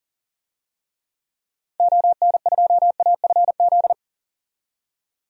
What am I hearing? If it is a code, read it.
Morse code, heard as a keyed tone, about 40 wpm, 700 Hz. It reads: ON2AFZ
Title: DXCC-100 Call Signs - 6 Characters - 40wpm